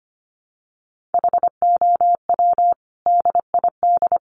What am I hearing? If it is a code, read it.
Morse code, heard as a keyed tone, about 25 wpm, 700 Hz. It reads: HOW DID
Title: Sentences from Top 400 Words 25wpm